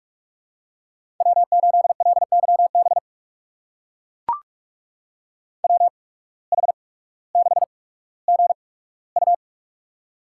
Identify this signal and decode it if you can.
Morse code, heard as a keyed tone, about 44 wpm, spaced as for 19 wpm, 700 Hz. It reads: W8LYB WH6GU
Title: USA Call Signs - 5 Characters - 8x Character Spacing - 45wpm